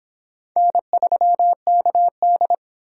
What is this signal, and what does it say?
Morse code, heard as a keyed tone, about 32 wpm, 700 Hz. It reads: N3KD